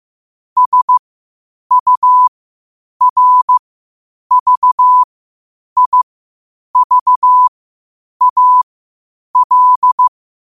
Morse code, heard as a keyed tone, about 15 wpm, spaced as for 8 wpm, 1000 Hz. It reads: SURVIVAL
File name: 8wpm